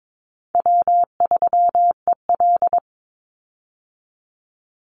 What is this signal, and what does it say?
Morse code, heard as a keyed tone, about 22 wpm, 700 Hz. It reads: W3EL
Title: Easy Call Signs - 22wpm